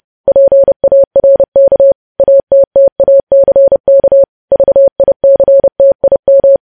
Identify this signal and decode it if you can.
Morse code, heard as a keyed tone, about 30 wpm, 550 Hz. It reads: PARK ATTACK VICTIM
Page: news headlines in morse code at 30 wpm